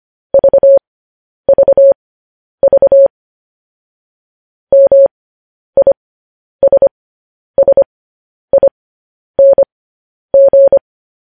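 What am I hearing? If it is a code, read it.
Morse code, heard as a keyed tone, about 25 wpm, spaced as for 10 wpm, 550 Hz. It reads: VVV MISSING
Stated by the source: Morse code practice sending news headlines at 10 wpm.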